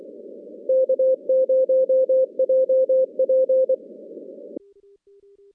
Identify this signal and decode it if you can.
Morse code, heard as a keyed tone, about 24 wpm, 520 Hz. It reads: K0JP
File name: contests / CW-SS-k5tr- -2100